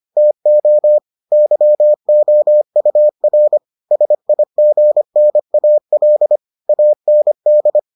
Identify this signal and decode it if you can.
Morse code, heard as a keyed tone, about 25 wpm, 600 Hz. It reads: TO YOUR SIGNAL AND